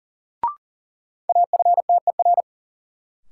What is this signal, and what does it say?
Morse code, heard as a keyed tone, about 40 wpm, 700 Hz. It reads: AFTER
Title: Sentences from Top 200 Words - New Word Drill - 40wpm